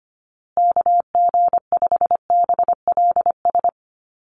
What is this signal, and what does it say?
Morse code, heard as a keyed tone, about 25 wpm, 700 Hz. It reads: KG5BLS
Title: USA Call Signs Repeated 2x 25wpm